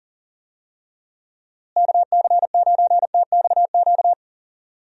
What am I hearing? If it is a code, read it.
Morse code, heard as a keyed tone, about 40 wpm, 700 Hz. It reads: KC9TXQ
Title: USA Call Signs - 6 Characters - Repeated 2x - 40wpm